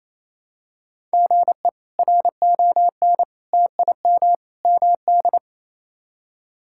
Morse code, heard as a keyed tone, about 28 wpm, 700 Hz. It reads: GE RON TIM MD
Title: K1USN SST QSO Practice - 28wpm